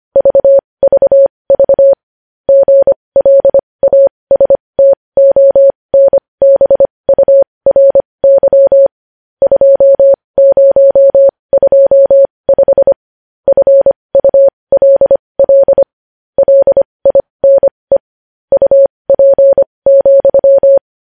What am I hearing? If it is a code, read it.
Morse code, heard as a keyed tone, about 25 wpm, spaced as for 20 wpm, 550 Hz. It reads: VVV GLASTONBURY 2025 FULL LINE UP,
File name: podcast / News headlines-1x-25x20